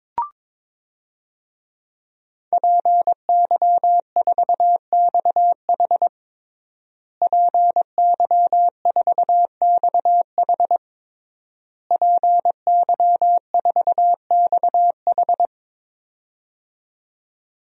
Morse code, heard as a keyed tone, about 22 wpm, 700 Hz. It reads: PY4XH PY4XH PY4XH
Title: DXCC-100 Call Signs - 5 Characters - Repeated 3x - 22wpm